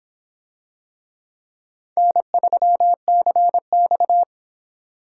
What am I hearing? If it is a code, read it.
Morse code, heard as a keyed tone, about 25 wpm, 700 Hz. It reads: N3CX